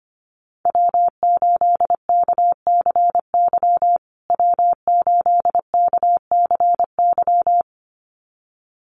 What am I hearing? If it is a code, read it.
Morse code, heard as a keyed tone, about 25 wpm, 700 Hz. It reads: W8KCY W8KCY
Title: USA Call Signs Repeated 2x 25wpm